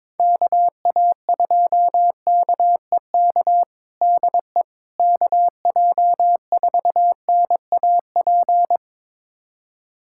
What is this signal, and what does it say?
Morse code, heard as a keyed tone, about 22 wpm, 700 Hz. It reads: KA2KEK DE KJ4NAP
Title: USA Callsign DE Callsign - 6 Characters - 22wpm